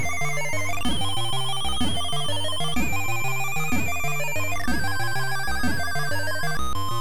There is no code.